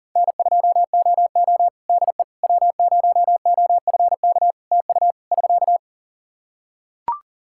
Morse code, heard as a keyed tone, about 40 wpm, 700 Hz. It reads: N2OO DE W0OFK TU <SK>
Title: K1USN SST QSO Practice 40wpm